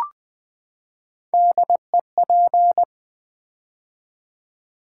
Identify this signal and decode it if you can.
Morse code, heard as a keyed tone, about 21 wpm, 700 Hz. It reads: DEP
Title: Build A Word - 3 Letters At A Time - 6-Letter Words - 20wpm